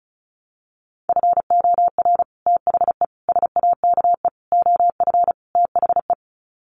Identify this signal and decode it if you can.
Morse code, heard as a keyed tone, about 34 wpm, 700 Hz. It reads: FOR THE SAKE OF THE
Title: Sets of 5 Words Encore 35wpm